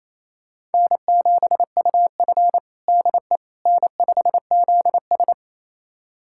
Morse code, heard as a keyed tone, about 28 wpm, 700 Hz. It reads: N7UF DE N5ZS
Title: USA Callsign DE Callsign - 4 Characters - 28wpm